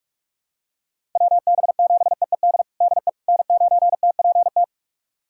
Morse code, heard as a keyed tone, about 46 wpm, 700 Hz. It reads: WB7EED DE N9TPT
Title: USA Callsign DE Callsign 45wpm